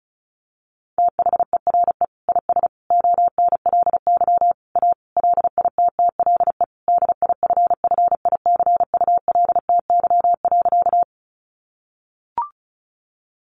Morse code, heard as a keyed tone, about 34 wpm, 700 Hz. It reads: THERE IS ONLY A LITTLE DIFFICULTY.
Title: Sentences from Top 1500 Words 35wpm